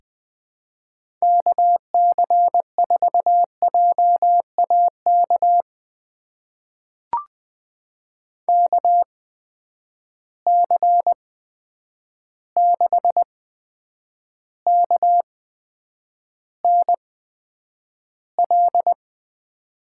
Morse code, heard as a keyed tone, about 20 wpm, spaced as for 9 wpm, 700 Hz. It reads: KC4JAK KC6KNL
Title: USA Call Signs - 6 Characters - 8x Character Spacing - 20wpm